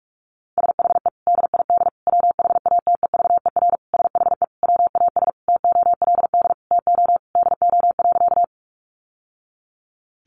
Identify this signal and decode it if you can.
Morse code, heard as a keyed tone, about 44 wpm, 700 Hz. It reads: SHE DID WHATEVER SHE WAS TOLD TO DO.
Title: Sentences from Top 1100 Words 45wpm